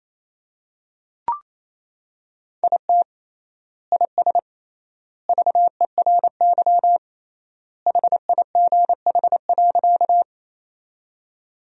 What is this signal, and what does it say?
Morse code, heard as a keyed tone, about 28 wpm, 700 Hz. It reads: IT IS VERY HIGH.
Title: 3x Word-Spacing - Sentences from Top 200 Words - 28wpm